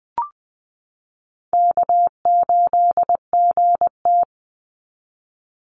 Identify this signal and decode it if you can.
Morse code, heard as a keyed tone, about 20 wpm, 700 Hz. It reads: K8GT
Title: Call Signs - 20wpm